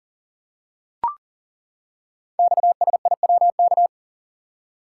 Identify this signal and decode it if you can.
Morse code, heard as a keyed tone, about 41 wpm, 700 Hz. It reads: XSIWK
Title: Code Groups - 5 Letters - 40wpm